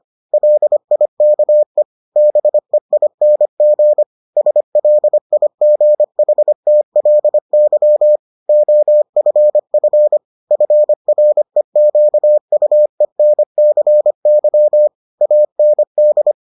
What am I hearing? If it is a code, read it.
Morse code, heard as a keyed tone, about 25 wpm, 600 Hz. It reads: LIKE BEING SLIGHTLY OFF FREQUENCY AND